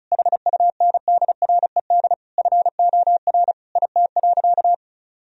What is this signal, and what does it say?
Morse code, heard as a keyed tone, about 36 wpm, 700 Hz. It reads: HUNDRED FOR IT.
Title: Sentences from Top 1000 Words - Review Part 2 - 35wpm